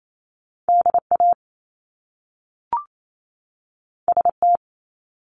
Morse code, heard as a keyed tone, about 28 wpm, 700 Hz. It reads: DA ST
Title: Top Two Letter Combinations in Words 28wpm